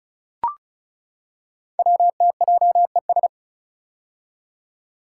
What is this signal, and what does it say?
Morse code, heard as a keyed tone, about 36 wpm, 700 Hz. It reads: WTJES
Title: Code Groups - 5 Letters - 35wpm